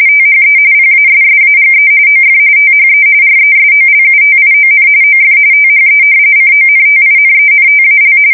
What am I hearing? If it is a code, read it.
RTTY, about 45.45 baud. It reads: THE PASSAGE OF THE RESOLUTION OF THE CONTINENTAL
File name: fsk170 / usnavy1